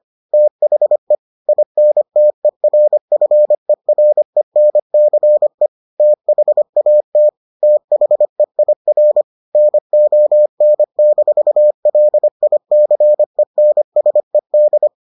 Morse code, heard as a keyed tone, about 25 wpm, 600 Hz. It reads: THE INTERFERENCE THAT THEIR NON-LICENSED